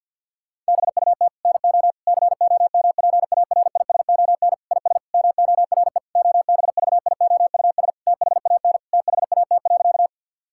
Morse code, heard as a keyed tone, about 52 wpm, 700 Hz. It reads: BUT NO COMPARISON IS MORE OBVIOUS THAN THAT.
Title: Sentences from Top 1600 Words 50wpm